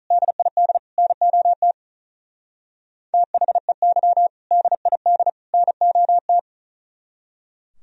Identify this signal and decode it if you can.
Morse code, heard as a keyed tone, about 42 wpm, 700 Hz. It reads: DID NOT THEY DID NOT
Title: Speed-Racing - Sets of 3 Words 35wpm